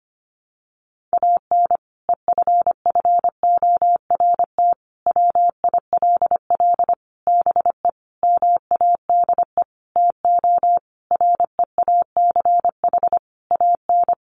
Morse code, heard as a keyed tone, about 25 wpm, 700 Hz. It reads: AN EFFORT WILL BE MADE TO REACH AN
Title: Sentences from Top 1500 Words 25wpm